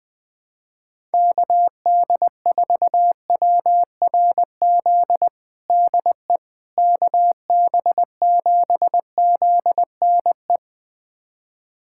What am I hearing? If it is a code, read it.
Morse code, heard as a keyed tone, about 20 wpm, 700 Hz. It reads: KD4WRZ DE KB7ZNE
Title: USA Callsign DE Callsign - 6 Characters - 20wpm